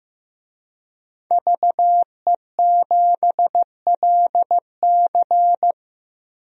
Morse code, heard as a keyed tone, about 15 wpm, 700 Hz. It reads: VE7LC
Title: DXCC-100 Call Signs - 5 Characters - 15wpm